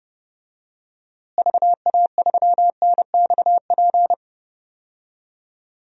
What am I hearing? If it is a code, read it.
Morse code, heard as a keyed tone, about 30 wpm, 700 Hz. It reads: VA3NXP
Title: DXCC-100 Call Signs - 6 Characters - Repeated 2x - 30wpm